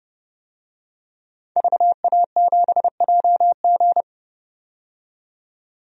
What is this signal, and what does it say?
Morse code, heard as a keyed tone, about 31 wpm, 700 Hz. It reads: VA7JG
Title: DXCC-100 Call Signs - 5 Characters - 30wpm